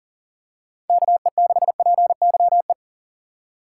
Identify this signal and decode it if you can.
Morse code, heard as a keyed tone, about 40 wpm, 700 Hz. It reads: KE6PYE
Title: USA Call Signs - 6 Characters - 4x Character Spacing - 40wpm